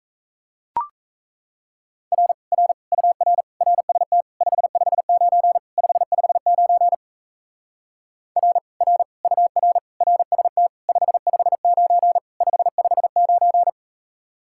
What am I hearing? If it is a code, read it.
Morse code, heard as a keyed tone, about 37 wpm, 700 Hz. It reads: R R UR RST 559 559 R R UR RST 559 559